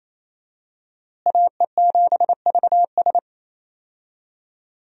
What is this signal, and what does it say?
Morse code, heard as a keyed tone, about 28 wpm, 700 Hz. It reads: AE7VS
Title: USA Call Signs - 5 Characters - Repeated 2x - 28wpm